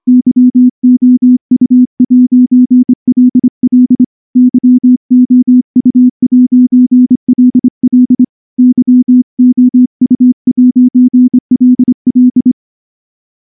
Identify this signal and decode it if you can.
Morse code, heard as a keyed tone, about 25 wpm, 250 Hz. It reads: YOU'LL YOU'LL YOU'LL
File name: morse / morse-training-25WPM